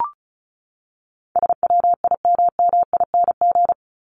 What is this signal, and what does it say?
Morse code, heard as a keyed tone, about 34 wpm, 700 Hz. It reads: SWIMMING
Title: Top 1501-1750 Words 35wpm